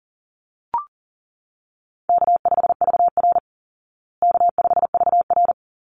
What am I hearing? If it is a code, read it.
Morse code, heard as a keyed tone, about 40 wpm, 700 Hz. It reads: K5VR K5VR
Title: Call Signs Repeated 3x - 40wpm